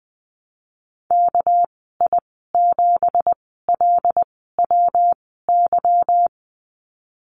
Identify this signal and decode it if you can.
Morse code, heard as a keyed tone, about 20 wpm, spaced as for 15 wpm, 700 Hz. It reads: KI7LWY
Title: 207 Lesson 33g - Letter j - Call Signs 20wpm